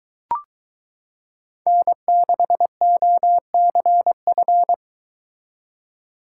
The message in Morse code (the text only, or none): N6OCF